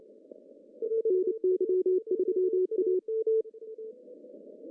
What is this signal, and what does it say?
Morse code, heard as a keyed tone, about 29 wpm, 370 Hz. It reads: NY3A
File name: CW / cqww-cw-k5tr